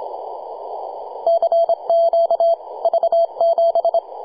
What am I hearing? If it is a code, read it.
Morse code, heard as a keyed tone, about 25 wpm, 670 Hz. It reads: CQ V7
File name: posts / 2015-sept-26-2040z-v73a-21-025-cw